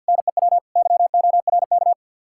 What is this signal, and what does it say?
Morse code, heard as a keyed tone, about 62 wpm, 700 Hz. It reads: NEW YORK
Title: Speed-Racing Common QSO Phrases - 50wpm